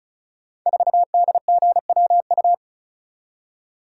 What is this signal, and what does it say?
Morse code, heard as a keyed tone, about 36 wpm, 700 Hz. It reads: 4DGWU
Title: Code Groups - 5 Letter:Numbers - 35wpm